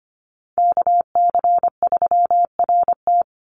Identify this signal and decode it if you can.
Morse code, heard as a keyed tone, about 25 wpm, 700 Hz. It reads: KC3RT
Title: NA SOTA Chaser Callsigns 25wpm